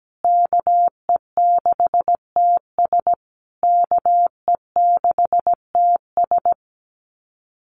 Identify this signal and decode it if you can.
Morse code, heard as a keyed tone, about 17 wpm, 700 Hz. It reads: KE6TS KE6TS
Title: USA Call Signs Repeated 3x 17wpm